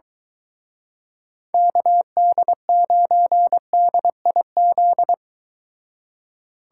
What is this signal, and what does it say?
Morse code, heard as a keyed tone, about 24 wpm, 700 Hz. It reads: KD9DIZ